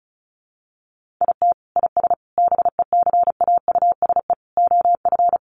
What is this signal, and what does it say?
Morse code, heard as a keyed tone, about 34 wpm, 700 Hz. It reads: IT IS BECAUSE OF
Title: Sentences from Top 600 Words 35wpm